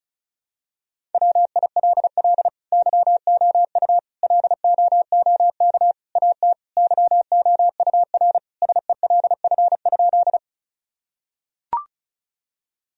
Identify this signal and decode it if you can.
Morse code, heard as a keyed tone, about 34 wpm, 700 Hz. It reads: WILL YOU LOOK AT YOURSELF?
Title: Sentences from Top 1000 Words - Review Part 1 - 35wpm